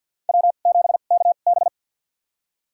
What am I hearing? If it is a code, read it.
Morse code, heard as a keyed tone, about 53 wpm, spaced as for 37 wpm, 700 Hz. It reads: W7KB
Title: USA Call Signs - 2x Character Spacing - 50wpm